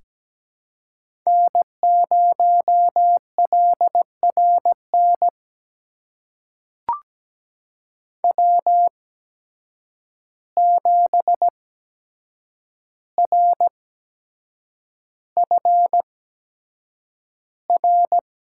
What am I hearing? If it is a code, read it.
Morse code, heard as a keyed tone, about 17 wpm, spaced as for 7 wpm, 700 Hz. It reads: N0LRN W7RFR
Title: USA Call Signs - 5 Characters - 8x Character Spacing - 17wpm